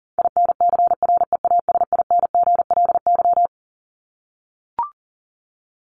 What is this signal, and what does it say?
Morse code, heard as a keyed tone, about 40 wpm, 700 Hz. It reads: INCREASINGLY
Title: LY Suffix - Top 100 Words with Base Words 40wpm